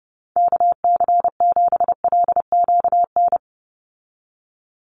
Morse code, heard as a keyed tone, about 29 wpm, 700 Hz. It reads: KC7LQN
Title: USA Call Signs 30wpm